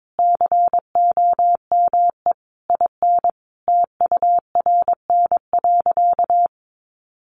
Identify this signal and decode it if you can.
Morse code, heard as a keyed tone, about 22 wpm, 700 Hz. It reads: COME IN TURN.
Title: Sentences from Top 1500 Words 22wpm